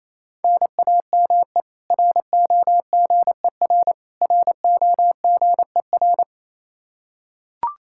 Morse code, heard as a keyed tone, about 28 wpm, 700 Hz. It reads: NAME ROGER ROGER
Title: Practice QSO - Exchange 1 - Beginning - 28wpm